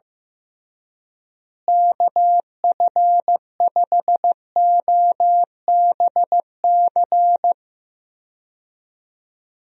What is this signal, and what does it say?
Morse code, heard as a keyed tone, about 16 wpm, 700 Hz. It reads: KF5OBC
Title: USA Call Signs - 6 Characters - Repeated 2x - 15wpm